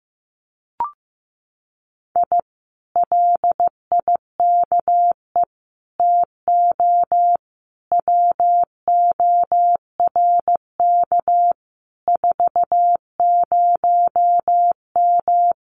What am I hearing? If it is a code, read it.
Morse code, heard as a keyed tone, about 15 wpm, 700 Hz. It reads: I LIKE TO WORK 40M
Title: Common QSO Phrases Encore 15wpm